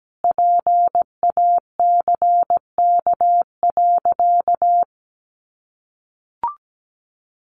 Morse code, heard as a keyed tone, about 17 wpm, 700 Hz. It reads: PACK.
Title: Sentences from Top 1300 Words 17wpm